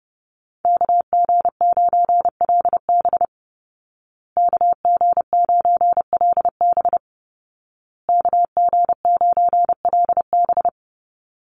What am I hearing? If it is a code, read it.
Morse code, heard as a keyed tone, about 30 wpm, 700 Hz. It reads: KG9LB KG9LB KG9LB
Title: Call Signs Repeated 2x - 30wpm